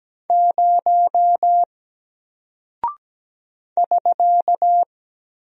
Morse code, heard as a keyed tone, about 17 wpm, 700 Hz. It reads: 0 <SK>
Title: Single Character (Letter:Number:Punctuation:Prosign) Alt 17wpm